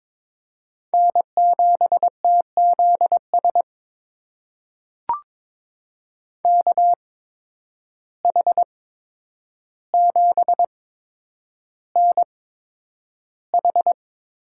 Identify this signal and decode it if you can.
Morse code, heard as a keyed tone, about 21 wpm, spaced as for 9 wpm, 700 Hz. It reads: N7TZS KH7NH
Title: USA Call Signs - 5 Characters - 8x Character Spacing - 22wpm